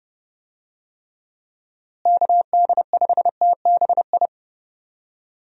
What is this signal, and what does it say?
Morse code, heard as a keyed tone, about 30 wpm, 700 Hz. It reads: KD5TBI
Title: USA Call Signs - 6 Characters - Repeated 2x - 30wpm